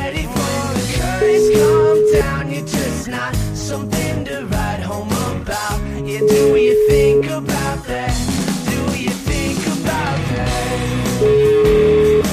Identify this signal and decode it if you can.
Ringback tone (Europe, 425 Hz), with noise.